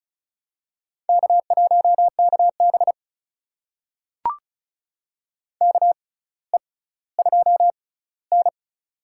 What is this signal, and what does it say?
Morse code, heard as a keyed tone, about 36 wpm, spaced as for 18 wpm, 700 Hz. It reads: K1KB KE2N
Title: USA Call Signs - 4 Characters - 6x Character Spacing - 35wpm